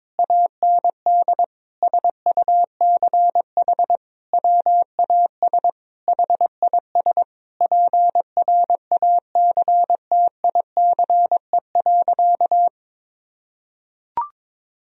Morse code, heard as a keyed tone, about 22 wpm, 700 Hz. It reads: AND SUCH WAS HIS PRACTICE.
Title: Sentences from Top 1000 Words - Review Part 4 - 22wpm